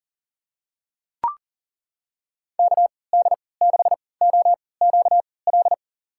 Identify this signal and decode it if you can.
Morse code, heard as a keyed tone, about 40 wpm, spaced as for 23 wpm, 700 Hz. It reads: KD6OQL
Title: USA Call Signs - 6 Characters - 3x Character Spacing - 40wpm